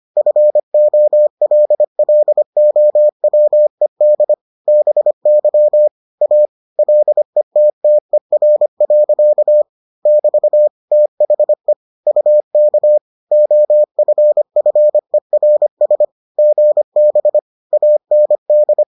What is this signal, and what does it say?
Morse code, heard as a keyed tone, about 25 wpm, 600 Hz. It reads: FOLLOWED BY A LETTER. <BT> THE UK OFFERS GB AND